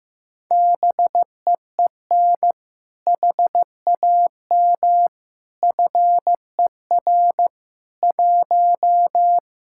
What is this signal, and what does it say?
Morse code, heard as a keyed tone, about 15 wpm, 700 Hz. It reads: BEEN HAM FER 1
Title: Speed-Racing Common QSO Phrases Encore - 15wpm